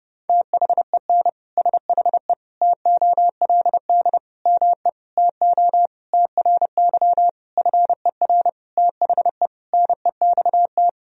Morse code, heard as a keyed tone, about 30 wpm, 700 Hz. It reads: THEN SHE TOLD ME TO TRY FER THE NEXT
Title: Sentences from Top 400 Words and Abbreviations 30wpm